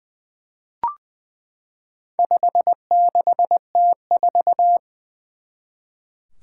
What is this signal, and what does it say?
Morse code, heard as a keyed tone, about 20 wpm, 700 Hz. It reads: 56T4
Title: Four Challenging Letter-Number Combinations 20wpm